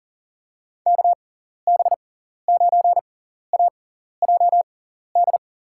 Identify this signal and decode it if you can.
Morse code, heard as a keyed tone, about 40 wpm, spaced as for 14 wpm, 700 Hz. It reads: KB9AJD
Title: USA Call Signs - 6x Character Spacing - 40wpm